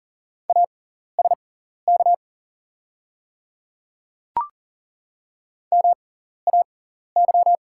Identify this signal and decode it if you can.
Morse code, heard as a keyed tone, about 40 wpm, spaced as for 14 wpm, 690 Hz. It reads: ASK MAY
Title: 6x Character Spacing - Top 200 Words - 40wpm